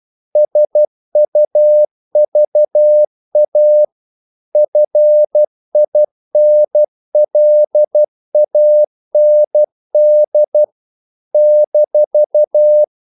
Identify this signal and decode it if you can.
Morse code, heard as a keyed tone, about 12 wpm, 600 Hz. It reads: SUVA FINLAND -
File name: CW / Practice / NationalCapitals12